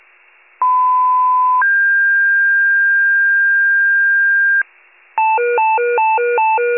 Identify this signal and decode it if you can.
Two-tone page, heard as a one-second tone, then a three-second tone, about 1010 then 1600 Hz. It is heard from 0.6 to 4.6 s.
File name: pager / 94 RESCUE DUTY / wcalert